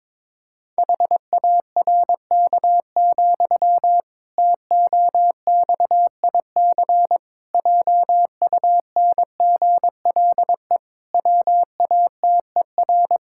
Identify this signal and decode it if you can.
Morse code, heard as a keyed tone, about 22 wpm, 700 Hz. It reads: HARK, TOXIC JUNGLE WATER
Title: Pangram Challenge v2 - 22wpm